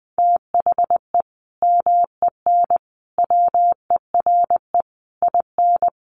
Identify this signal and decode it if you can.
Morse code, heard as a keyed tone, about 20 wpm, 700 Hz. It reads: THE MEN WERE IN
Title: Sentences from Top 1400 Words 20wpm